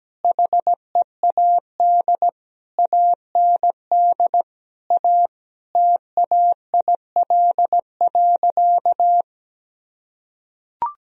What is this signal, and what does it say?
Morse code, heard as a keyed tone, about 17 wpm, 700 Hz. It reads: HEAD AND A TAIL.
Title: Sentences from Top 1000 Words - Review Part 3 - 17wpm